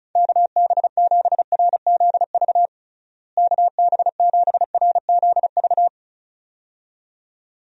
Morse code, heard as a keyed tone, about 36 wpm, 700 Hz. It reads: KB7RZV KB7RZV
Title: USA Call Signs - 6 Characters - Repeated 2x - 35wpm